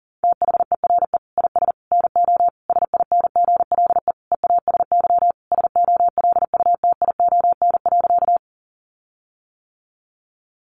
Morse code, heard as a keyed tone, about 37 wpm, 700 Hz. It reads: THERE IS NO SINGLE EASY SOLUTION.
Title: Sentences from Top 1300 Words 40wpm